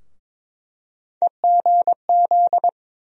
Morse code, heard as a keyed tone, about 22 wpm, 700 Hz. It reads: EGZ
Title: USA Call Sign Postfixes - 3 Letters 22wpm